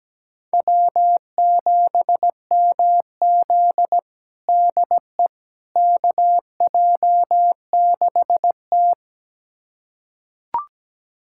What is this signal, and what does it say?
Morse code, heard as a keyed tone, about 17 wpm, 700 Hz. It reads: W7MZ DE KJ6T
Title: USA Callsign DE Callsign - 4 Characters - 17wpm